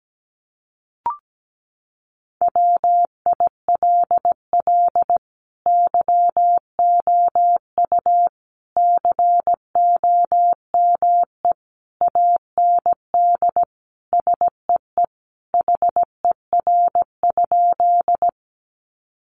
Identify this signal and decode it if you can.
Morse code, heard as a keyed tone, about 17 wpm, 700 Hz. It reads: WILL YOU COME AND SEE HER?
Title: Sentences from Top 100 Words 17wpm